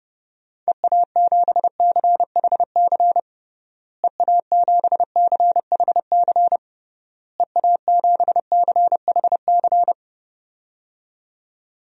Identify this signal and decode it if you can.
Morse code, heard as a keyed tone, about 30 wpm, 700 Hz. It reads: EA7CHC EA7CHC EA7CHC
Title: DXCC-100 Call Signs - 6 Characters - Repeated 3x - 30wpm